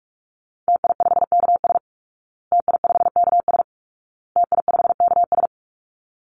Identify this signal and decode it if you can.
Morse code, heard as a keyed tone, about 44 wpm, 700 Hz. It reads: TI5KS TI5KS TI5KS
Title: Call Signs Repeated 2x - 45wpm